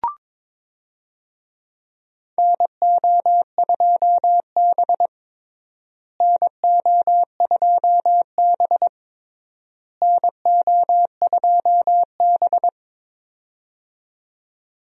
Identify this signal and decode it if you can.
Morse code, heard as a keyed tone, about 22 wpm, 700 Hz. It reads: NO2B NO2B NO2B
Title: USA Call Signs - 4 Characters - Repeated 3x - 22wpm